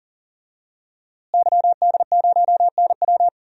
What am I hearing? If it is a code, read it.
Morse code, heard as a keyed tone, about 37 wpm, 700 Hz. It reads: YD0NW
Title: DXCC-100 Call Signs - 5 Characters - Repeated 3x - 40wpm